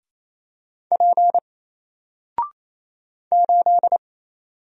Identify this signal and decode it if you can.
Morse code, heard as a keyed tone, about 27 wpm, 700 Hz. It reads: P 8
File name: Single-Letter-Number-28wpm